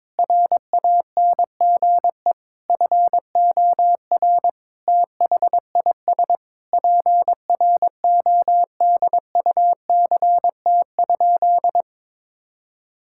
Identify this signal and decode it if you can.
Morse code, heard as a keyed tone, about 22 wpm, 700 Hz. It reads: RANGE FOR THIS PRODUCT?
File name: Sentences with 9-Word Phrases - 22wpm